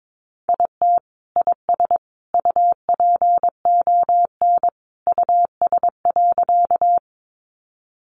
Morse code, heard as a keyed tone, about 22 wpm, 700 Hz. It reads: IT IS UPON US.
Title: Sentences from Top 1100 Words 22wpm